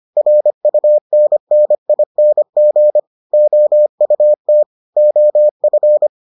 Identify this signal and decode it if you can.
Morse code, heard as a keyed tone, about 25 wpm, 600 Hz. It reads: RUNNING OUT OF